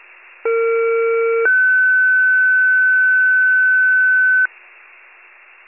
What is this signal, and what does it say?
Two-tone page, heard as a one-second tone, then a three-second tone, about 460 then 1500 Hz. It is heard from 0.5 to 4.5 s.
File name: pager / 21 FIRE / wcalert